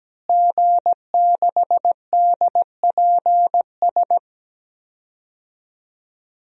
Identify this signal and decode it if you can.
Morse code, heard as a keyed tone, about 17 wpm, 700 Hz. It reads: G6DPS
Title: DXCC-100 Call Signs - 5 Characters - Repeated 2x - 17wpm